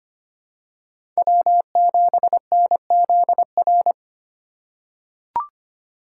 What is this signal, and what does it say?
Morse code, heard as a keyed tone, about 25 wpm, 700 Hz. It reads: W7NZR
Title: USA Call Signs - 5 Characters - 2x Character Spacing - 25wpm